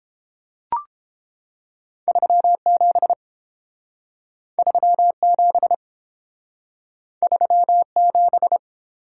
Morse code, heard as a keyed tone, about 27 wpm, 700 Hz. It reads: 37 37 37
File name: Speed-Racing - Two Letter-Number Combinations - 22wpm